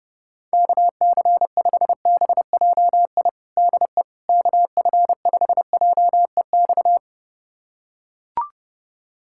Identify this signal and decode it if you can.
Morse code, heard as a keyed tone, about 30 wpm, 700 Hz. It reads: KC5BJI DE KF5JEX
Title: USA Callsign DE Callsign - 6 Characters - 30wpm